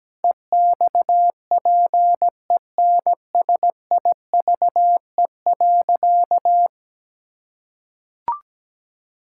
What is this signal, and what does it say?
Morse code, heard as a keyed tone, about 17 wpm, 700 Hz. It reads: EXPENSIVE.
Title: Sentences from Top 1000 Words - Review Part 6 - 17wpm